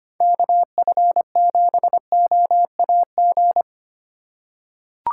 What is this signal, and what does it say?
Morse code, heard as a keyed tone, about 25 wpm, 700 Hz. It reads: KF7OAG
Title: USA Call Signs - 6 Characters - 2x Character Spacing - 25wpm